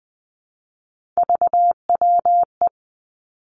Morse code, heard as a keyed tone, about 20 wpm, 700 Hz. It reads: VWE
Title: Three Challenging Letter-Number Combinations 20wpm